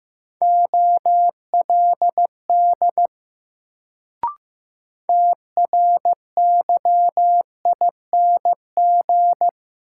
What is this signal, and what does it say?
Morse code, heard as a keyed tone, about 15 wpm, 700 Hz. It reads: OLD TRYING
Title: Sets of 2 Words - New Word Drill - 15wpm